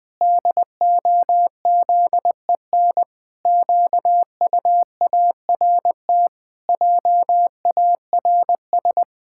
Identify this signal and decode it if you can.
Morse code, heard as a keyed tone, about 19 wpm, 700 Hz. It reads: DOZEN QUART JARS
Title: Pangram Challenge v2 - 20wpm